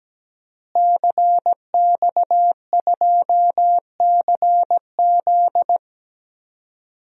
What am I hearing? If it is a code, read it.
Morse code, heard as a keyed tone, about 17 wpm, 700 Hz. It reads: CX2CZ
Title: DXCC-100 Call Signs - 5 Characters - Repeated 3x - 17wpm